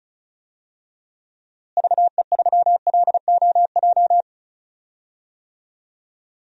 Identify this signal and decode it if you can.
Morse code, heard as a keyed tone, about 34 wpm, 700 Hz. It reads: VE3LOJ
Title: DXCC-100 Call Signs - 6 Characters - Repeated 3x - 35wpm